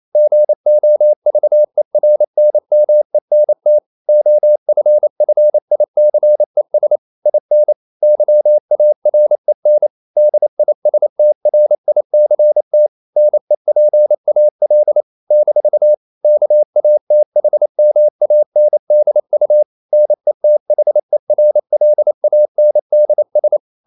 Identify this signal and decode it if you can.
Morse code, heard as a keyed tone, about 28 wpm, 600 Hz. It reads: GOVERNMENT OFFICES IN YAREN DISTRICT NEPAL - KATHMANDU NETHERLANDS
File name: CW / Practice / NationalCapitals28